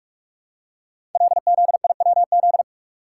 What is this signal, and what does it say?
Morse code, heard as a keyed tone, about 44 wpm, 700 Hz. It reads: LZIWZ
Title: Code Groups - 5 Letters - 45wpm